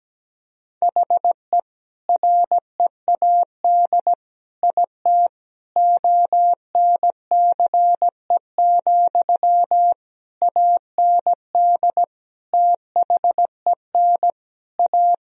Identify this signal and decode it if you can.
Morse code, heard as a keyed tone, about 17 wpm, 700 Hz. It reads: HE READ IT ONCE, AND THEN A
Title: Sentences from Top 1000 Words - Review Part 1 - 17wpm